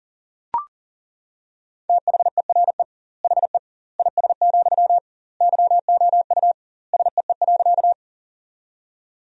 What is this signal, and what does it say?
Morse code, heard as a keyed tone, about 37 wpm, 700 Hz. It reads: THERE HE IS, YOU SEE.
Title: 2x Word-Spacing - Sentences from Top 100 Words - 40wpm